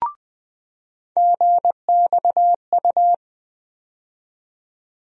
Morse code, heard as a keyed tone, about 20 wpm, 700 Hz. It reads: GXU
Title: USA Call Sign Postfixes - 3 Letters 20wpm